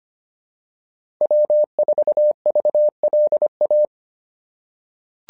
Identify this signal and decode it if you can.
Morse code, heard as a keyed tone, about 25 wpm, 600 Hz. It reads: W4VLA
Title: SOTA QSO Practice 25wpm